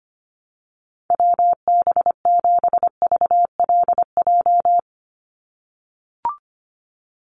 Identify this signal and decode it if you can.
Morse code, heard as a keyed tone, about 25 wpm, 700 Hz. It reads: WB7VLJ
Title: USA Call Signs 25wpm